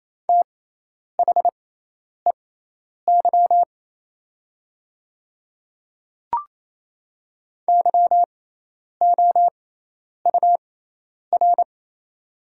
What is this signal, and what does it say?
Morse code, heard as a keyed tone, about 27 wpm, spaced as for 10 wpm, 700 Hz. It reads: THEY YOUR
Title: 6x Character Spacing - Top 200 Words - 28wpm